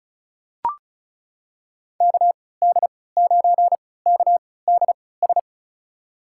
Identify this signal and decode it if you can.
Morse code, heard as a keyed tone, about 36 wpm, spaced as for 20 wpm, 700 Hz. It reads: KD9KDS
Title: USA Call Signs - 6 Characters - 3x Character Spacing - 35wpm